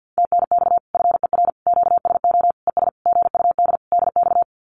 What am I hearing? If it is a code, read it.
Morse code, heard as a keyed tone, about 48 wpm, 700 Hz. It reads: TNX FER QSO ES GUD DX
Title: Common QSO Phrases Encore 50wpm